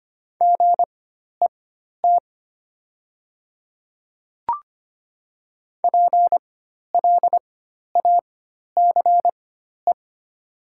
Morse code, heard as a keyed tone, about 25 wpm, spaced as for 12 wpm, 700 Hz. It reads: GET PLACE
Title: 4x Character Spacing - Top 200 Words - 25wpm